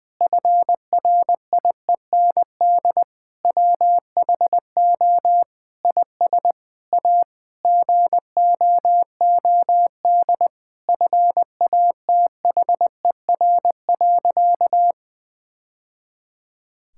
Morse code, heard as a keyed tone, about 20 wpm, 700 Hz. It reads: FRIEND WHO IS A GOOD FATHER.
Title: Sentences from Top 1000 Words - Review Part 5 - 20wpm